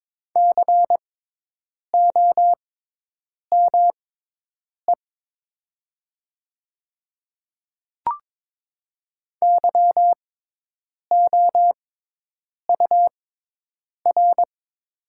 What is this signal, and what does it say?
Morse code, heard as a keyed tone, about 23 wpm, spaced as for 8 wpm, 700 Hz. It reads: COME YOUR
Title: 6x Character Spacing - Top 300 Words - 22wpm